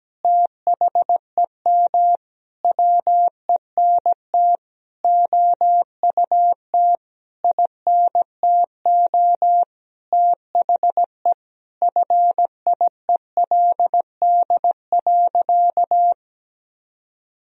Morse code, heard as a keyed tone, about 17 wpm, 700 Hz. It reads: THEM WENT OUT INTO THE FIELD.
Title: Sentences from Top 1000 Words - Review Part 2 - 17wpm